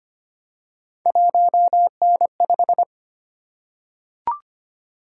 Morse code, heard as a keyed tone, about 25 wpm, 700 Hz. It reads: 1N5
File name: Three Letter-Number Combinations - 25wpm